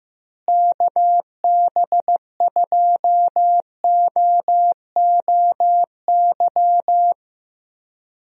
Morse code, heard as a keyed tone, about 15 wpm, 700 Hz. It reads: KB2OOY
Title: USA Call Signs - 6 Characters - Repeated 2x - 15wpm